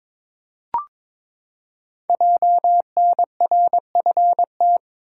Morse code, heard as a keyed tone, about 23 wpm, 700 Hz. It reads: JNRFT
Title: Code Groups - 5 Letter:Numbers - 22wpm